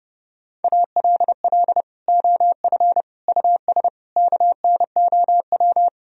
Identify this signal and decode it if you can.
Morse code, heard as a keyed tone, about 30 wpm, 700 Hz. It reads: ALL OF US KNOW
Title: Sentences from Top 1000 Words - Review Part 3 - 30wpm